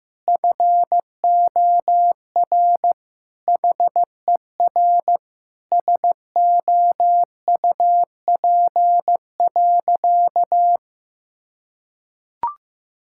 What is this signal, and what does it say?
Morse code, heard as a keyed tone, about 15 wpm, 700 Hz. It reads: FOR HER SOUP.
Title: Sentences from Top 1000 Words - Review Part 6 - 15wpm